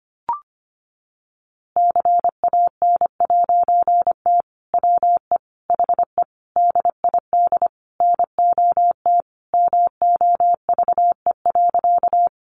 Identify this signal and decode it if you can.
Morse code, heard as a keyed tone, about 25 wpm, 700 Hz. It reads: CAN'T WE HE DID NOT MOVE.
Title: A Sound Of Thunder 25wpm